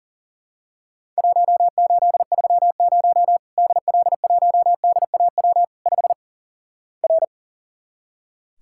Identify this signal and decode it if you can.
Morse code, heard as a keyed tone, about 37 wpm, 700 Hz. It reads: 1830 DL1DAW 5 R
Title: Worked All Europe DX Contest - QTC Exchange - 40wpm